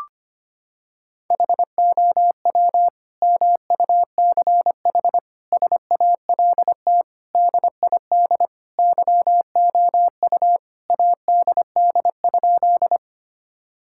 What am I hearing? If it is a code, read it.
Morse code, heard as a keyed tone, about 25 wpm, 700 Hz. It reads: HOW MUCH SALT DID YOU ADD?
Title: Sentences from Top 1000 Words - Review Part 5 - 25wpm